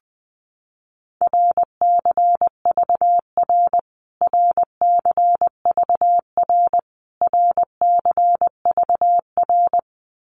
Morse code, heard as a keyed tone, about 20 wpm, 700 Hz. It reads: RCVR RCVR RCVR
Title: Abbreviations Repeated 3x - 20wpm